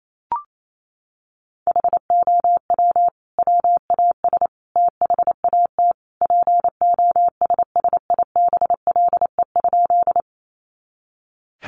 Morse code, heard as a keyed tone, about 28 wpm, 700 Hz. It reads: HOW WAS THAT POSSIBLE?
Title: Sentences from Top 500 Words 28wpm